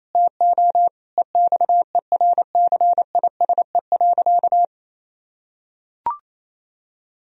Morse code, heard as a keyed tone, about 28 wpm, 700 Hz. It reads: TO EXERCISE.
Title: Sentences from Top 1000 Words - Review Part 4 - 28wpm